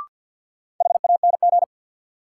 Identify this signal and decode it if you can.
Morse code, heard as a keyed tone, about 53 wpm, 700 Hz. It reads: HANG
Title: Sentences from Top 1600 Words - New Word Drill Alt - 50wpm